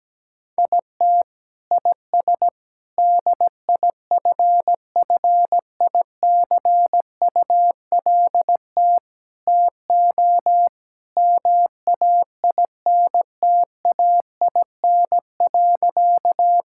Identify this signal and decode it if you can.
Morse code, heard as a keyed tone, about 17 wpm, 700 Hz. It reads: IT IS DIFFICULT TO MAINTAIN.
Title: Sentences from Top 1000 Words - Review Part 5 - 17wpm